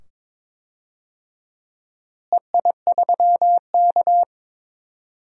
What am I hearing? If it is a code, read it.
Morse code, heard as a keyed tone, about 22 wpm, 700 Hz. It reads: EI3K
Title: DXCC-100 Call Signs - 4 Characters - Repeated 2x - 22wpm